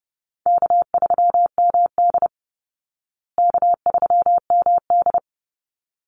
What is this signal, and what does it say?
Morse code, heard as a keyed tone, about 30 wpm, 700 Hz. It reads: K3MD K3MD
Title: Call Signs Repeated 3x - 30wpm